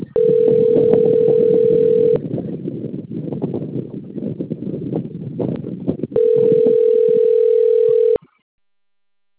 Ringback tone (North America), 440 + 480 Hz.